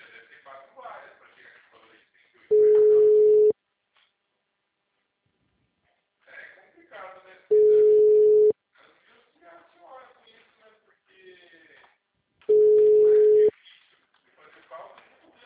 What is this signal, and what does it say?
Ringback tone (Europe, 425 Hz), with speech.